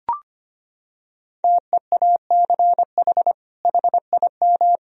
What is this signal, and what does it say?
Morse code, heard as a keyed tone, about 25 wpm, 700 Hz. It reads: TEACH HIM
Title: Sentences from Top 1000 Words - Review Part 3 - 25wpm